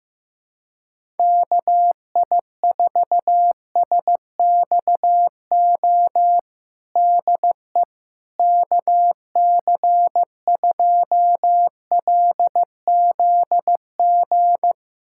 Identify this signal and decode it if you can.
Morse code, heard as a keyed tone, about 15 wpm, 700 Hz. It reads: KI4SXO DE KC2LZG
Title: USA Callsign DE Callsign 15wpm